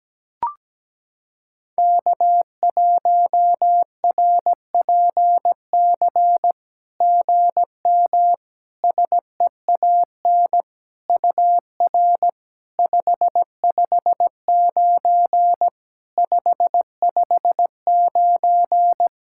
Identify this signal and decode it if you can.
Morse code, heard as a keyed tone, about 17 wpm, 690 Hz. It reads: K1RPC GM SEAN UR 559 559
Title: SOTA QSO Practice 17wpm